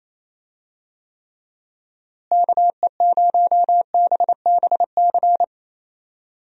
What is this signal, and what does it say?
Morse code, heard as a keyed tone, about 28 wpm, 700 Hz. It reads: KE0BBC
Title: USA Call Signs - 6 Characters - Repeated 2x - 28wpm